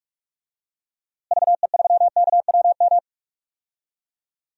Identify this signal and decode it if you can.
Morse code, heard as a keyed tone, about 42 wpm, 700 Hz. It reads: VE3KWM
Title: DXCC-100 Call Signs - 6 Characters - Repeated 3x - 45wpm